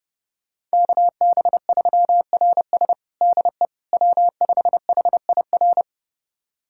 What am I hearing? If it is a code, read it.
Morse code, heard as a keyed tone, about 32 wpm, 700 Hz. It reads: KB3RS DE W5HIR
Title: USA Callsign DE Callsign 30wpm